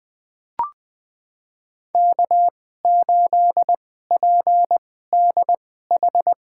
Morse code, heard as a keyed tone, about 20 wpm, spaced as for 14 wpm, 700 Hz. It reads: K8PDH
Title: USA Call Signs - 5 Characters - 2x Character Spacing - 20wpm